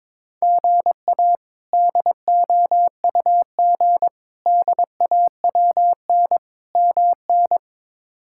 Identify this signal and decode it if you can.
Morse code, heard as a keyed tone, about 22 wpm, 700 Hz. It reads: GA DOUG DAWN MN
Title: K1USN SST QSO Practice - 22wpm